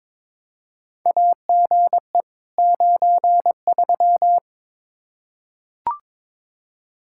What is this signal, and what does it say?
Morse code, heard as a keyed tone, about 22 wpm, 700 Hz. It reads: AGE 93
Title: K1USN SST QSO Practice 22wpm